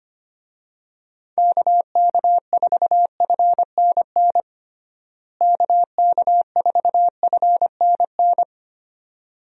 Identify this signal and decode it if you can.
Morse code, heard as a keyed tone, about 25 wpm, 700 Hz. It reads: KK4FNN KK4FNN
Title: USA Call Signs - 6 Characters - Repeated 2x - 25wpm